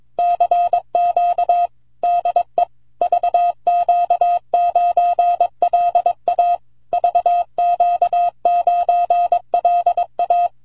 Morse code, heard as a keyed tone, about 22 wpm, 700 Hz. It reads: CQ DE VQ9LA VQ9LA
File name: vq9la / W3BGN 18Aug08 0108Z 80mtrs